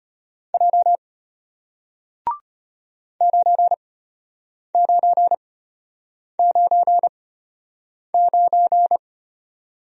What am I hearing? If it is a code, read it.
Morse code, heard as a keyed tone, about 34 wpm, spaced as for 8 wpm, 700 Hz. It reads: J 9999